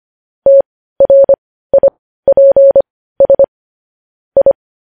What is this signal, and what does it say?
Morse code, heard as a keyed tone, about 25 wpm, spaced as for 15 wpm, 550 Hz. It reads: TRIPS I
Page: news headlines in morse code at 15 wpm